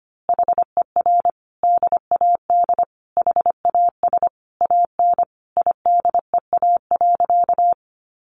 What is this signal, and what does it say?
Morse code, heard as a keyed tone, about 25 wpm, 700 Hz. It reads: HER DAD HAS AN IDEA.
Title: Easy Phrases 25wpm